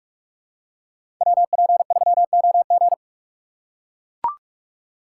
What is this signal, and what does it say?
Morse code, heard as a keyed tone, about 44 wpm, 700 Hz. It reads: WP3OG
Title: USA Call Signs - 5 Characters - 6x Character Spacing - 45wpm